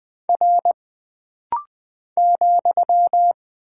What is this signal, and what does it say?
Morse code, heard as a keyed tone, about 20 wpm, 700 Hz. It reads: R ,
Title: Single Character (Letter:Number:Punctuation:Prosign) - Rapid-Fire - 20wpm